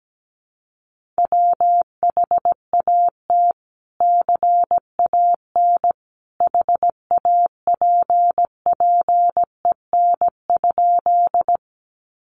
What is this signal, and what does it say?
Morse code, heard as a keyed tone, about 17 wpm, 700 Hz. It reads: WHAT CAN HAPPEN?
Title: Sentences from Top 400 Words 17wpm